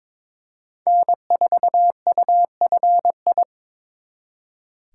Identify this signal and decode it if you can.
Morse code, heard as a keyed tone, about 26 wpm, 700 Hz. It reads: N4UFI
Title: Speed Racing - USA 1x3 Call Signs 22wpm